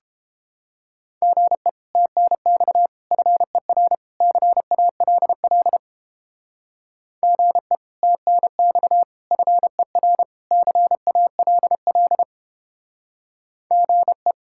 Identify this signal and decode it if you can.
Morse code, heard as a keyed tone, about 29 wpm, 700 Hz. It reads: GE TNX FER CALL GE TNX FER CALL GE